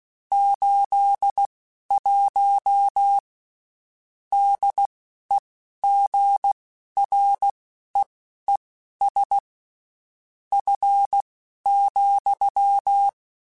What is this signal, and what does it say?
Morse code, heard as a keyed tone, about 16 wpm, spaced as for 12 wpm, 800 Hz. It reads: 81 DEGREES F,
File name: Spaced Characters / Spaced Characters-mp3s / Weather 5 12 sp @15